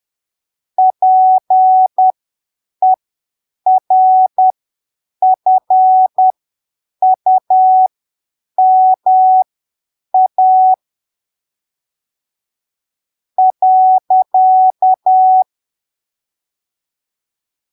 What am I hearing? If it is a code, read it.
Morse code, heard as a keyed tone, about 10 wpm, spaced as for 6 wpm, 750 Hz. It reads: PERFUMA.
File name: arquivos / texto / texto 20 10ppm